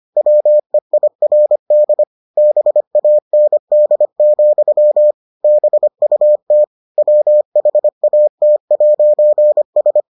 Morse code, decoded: WEIRD BAND, BUT WHAT'S

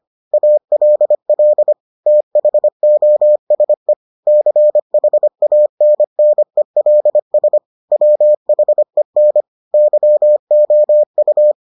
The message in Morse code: ALL THOSE CHANNELS WHEN YOU

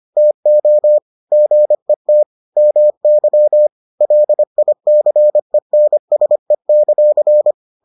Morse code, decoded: TO GET MY LICENSE;

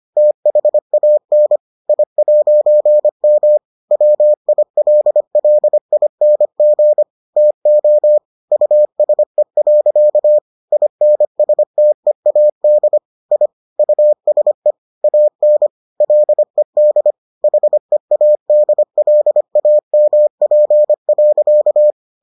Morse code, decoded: THAN I'M WILLING TO USE. INSTEAD I USE AN LED HEADLAMP.

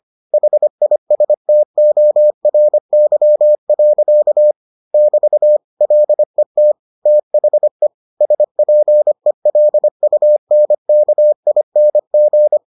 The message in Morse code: HISTORY. <BT> LET THE SPELUNKING

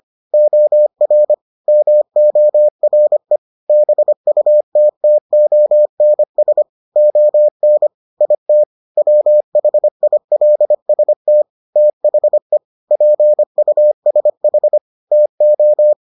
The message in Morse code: OR MORE BUTTONS ON IT WHILST THE PUSH TO

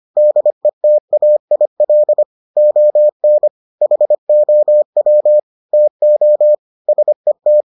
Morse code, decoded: DETAIL ON HOW TO SET

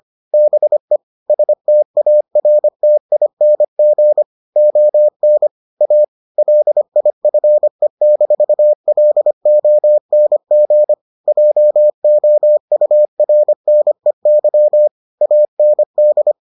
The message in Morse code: BE STARTING ON A LIFE-LONG JOURNEY AND

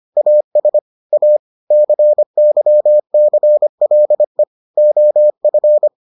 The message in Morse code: AS A CYCLE OF

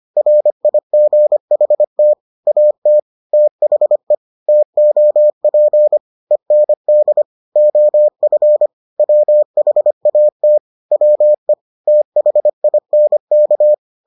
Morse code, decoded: RIGHT AT THE TOP END OF WHAT WE THINK

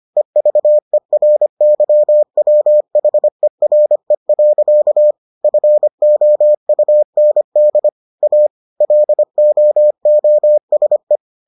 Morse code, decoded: EVERYWHERE. FOUND A LOOSE

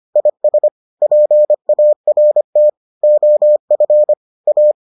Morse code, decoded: IS PART OF A

